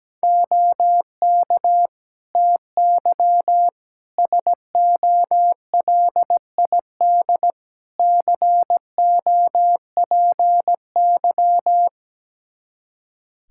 Morse code, heard as a keyed tone, about 17 wpm, 700 Hz. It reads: OK TY SOLID COPY
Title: K1USN SST QSO Practice 17wpm